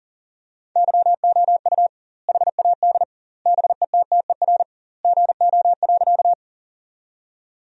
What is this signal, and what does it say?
Morse code, heard as a keyed tone, about 40 wpm, 700 Hz. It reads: YOU HAD BETTER GO.
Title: 2x Word-Spacing - Sentences from Top 400 Words - 40wpm